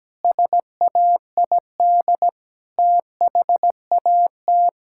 Morse code, heard as a keyed tone, about 17 wpm, 700 Hz. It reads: SAID THAT
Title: Pangram Challenge v2 - 17wpm